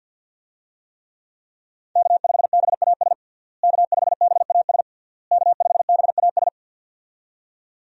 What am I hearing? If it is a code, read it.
Morse code, heard as a keyed tone, about 51 wpm, 700 Hz. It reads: K5BAS K5BAS K5BAS
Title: USA Call Signs - 5 Characters - Repeated 3x - 50wpm